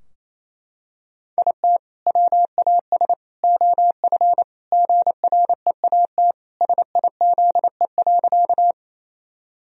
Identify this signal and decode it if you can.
Morse code, heard as a keyed tone, about 27 wpm, 700 Hz. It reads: IT WAS OF GREAT SIZE.
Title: Sentences from Top 1000 Words - Review Part 2 - 28wpm